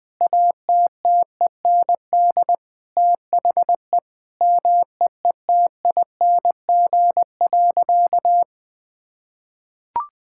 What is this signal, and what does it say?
Morse code, heard as a keyed tone, about 20 wpm, 700 Hz. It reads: ATTEND THE MEETING.
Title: Sentences with 6-Word Phrases - Spoken Beginning - 20wpm